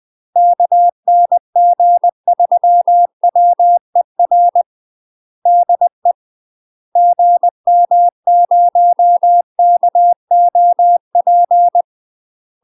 Morse code, heard as a keyed tone, about 20 wpm, 700 Hz. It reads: KN G3WER DE GM0KOP
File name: UK MASTER QSOs TRAINING / 20WPM / UK MASTER QSOs 0025